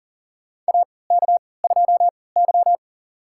Morse code, heard as a keyed tone, about 40 wpm, spaced as for 23 wpm, 700 Hz. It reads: AK2Y
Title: USA Call Signs - 4 Characters - 3x Character Spacing - 40wpm